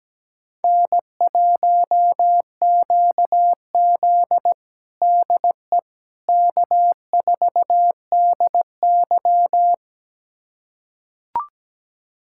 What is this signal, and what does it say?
Morse code, heard as a keyed tone, about 17 wpm, 700 Hz. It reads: N1QZ DE K4DY
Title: USA Callsign DE Callsign - 4 Characters - 17wpm